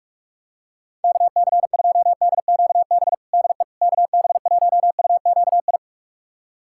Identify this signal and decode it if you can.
Morse code, heard as a keyed tone, about 46 wpm, 700 Hz. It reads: KC2DQB DE KB1UQI
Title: USA Callsign DE Callsign 45wpm